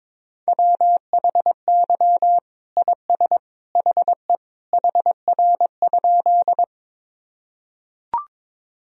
Morse code, heard as a keyed tone, about 21 wpm, 700 Hz. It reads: WHY IS HE HR?
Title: Sentences from Top 200 Words and Abbreviations 22wpm